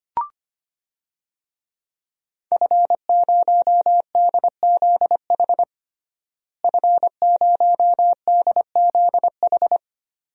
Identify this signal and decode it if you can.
Morse code, heard as a keyed tone, about 25 wpm, 700 Hz. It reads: F0DZH F0DZH
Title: DXCC-100 Call Signs - 5 Characters - Repeated 2x - 25wpm